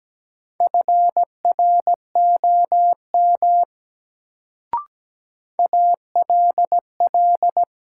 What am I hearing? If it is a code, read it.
Morse code, heard as a keyed tone, about 17 wpm, 700 Hz. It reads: FROM ALL
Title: Sentences from Top 100 Words - New Word Drill - 17wpm